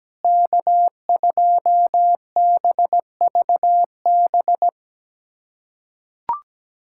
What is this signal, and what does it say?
Morse code, heard as a keyed tone, about 17 wpm, 700 Hz. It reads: K2BVB
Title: USA Callsign DE Callsign - 5 Characters - 17wpm